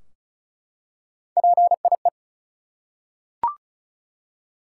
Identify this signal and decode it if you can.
Morse code, heard as a keyed tone, about 36 wpm, 700 Hz. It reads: PIE
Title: Three Letter Words - 35wpm